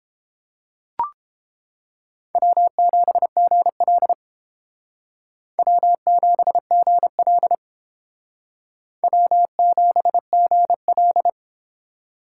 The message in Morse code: W7GL W7GL W7GL

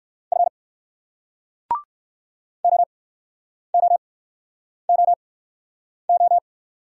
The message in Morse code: F O O O O

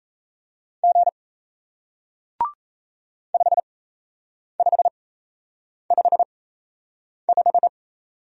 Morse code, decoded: G 5555

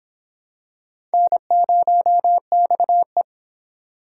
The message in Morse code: N0XE

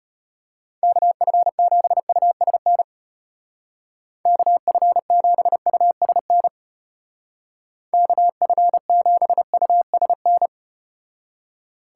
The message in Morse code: KF7USN KF7USN KF7USN

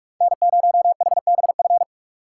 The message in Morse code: N0HBF

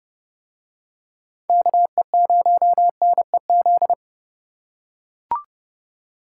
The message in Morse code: KE0NEZ

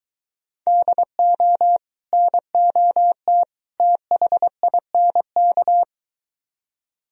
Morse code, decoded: DO NOT THINK

